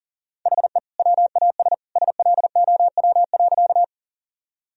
Morse code, heard as a keyed tone, about 40 wpm, 700 Hz. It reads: HE WAS SLOW.
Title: Sentences from Top 1000 Words - Review Part 2 - 40wpm